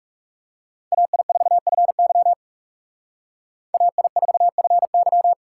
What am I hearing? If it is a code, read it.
Morse code, heard as a keyed tone, about 38 wpm, 700 Hz. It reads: AI4FY AI4FY